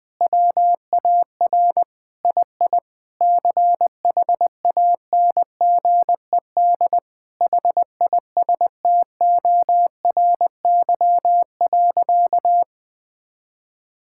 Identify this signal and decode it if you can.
Morse code, heard as a keyed tone, about 19 wpm, 700 Hz. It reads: WAR II CHANGED HISTORY.
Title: Sentences with 7-Word Phrases - 20wpm